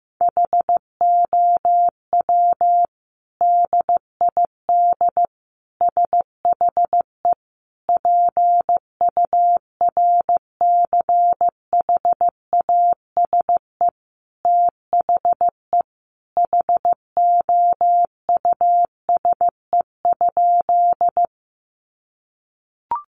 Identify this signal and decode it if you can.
Morse code, heard as a keyed tone, about 15 wpm, 700 Hz. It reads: HOW DID SHE PURCHASE THE HOUSE?
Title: Sentences from Top 800 Words 15wpm